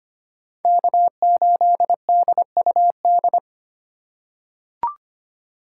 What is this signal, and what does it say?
Morse code, heard as a keyed tone, about 25 wpm, 700 Hz. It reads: K8DUD
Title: USA Call Signs - 3x Character Spacing - 25wpm